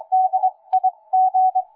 Morse code, heard as a keyed tone, about 23 wpm, spaced as for 19 wpm, 750 Hz. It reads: DIG
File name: cw forum / YV1DIG 1823 le 23-03-2009 05h09